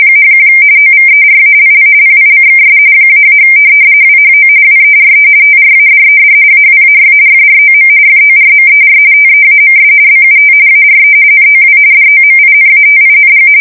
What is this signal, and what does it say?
RTTY, about 45.45 baud. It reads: #1 OVERALL PICKS -- WHO WERE TRADED AWAY FROM FORMER TEAMS BUT ARE NOW DOMINATING
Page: 170 Shift 2125/2295